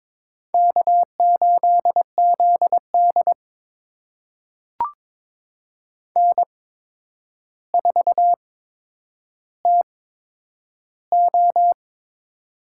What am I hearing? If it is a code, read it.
Morse code, heard as a keyed tone, about 22 wpm, spaced as for 9 wpm, 700 Hz. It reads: K8ZD N4TO
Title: USA Call Signs - 4 Characters - 8x Character Spacing - 22wpm